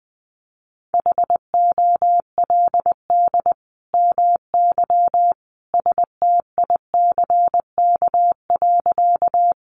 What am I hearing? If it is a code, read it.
Morse code, heard as a keyed tone, about 20 wpm, 700 Hz. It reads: HOLD MY STICK.
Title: Sentences from Top 1200 Words 20wpm